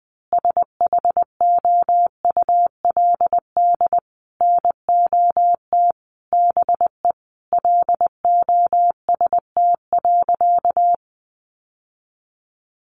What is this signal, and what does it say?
Morse code, heard as a keyed tone, about 20 wpm, 700 Hz. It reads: SHOULD NOT BE LOST.
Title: Sentences from Top 1400 Words 20wpm